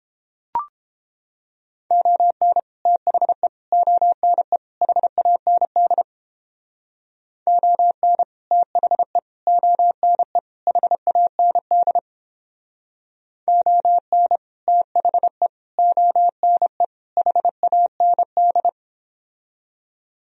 Morse code, heard as a keyed tone, about 29 wpm, 700 Hz. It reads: ON THE ONE HAND ON THE ONE HAND ON THE ONE HAND